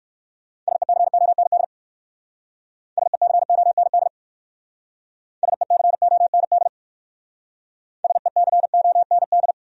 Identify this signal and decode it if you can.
Morse code, heard as a keyed tone, about 58 wpm, 700 Hz. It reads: SECOND SECOND SECOND SECOND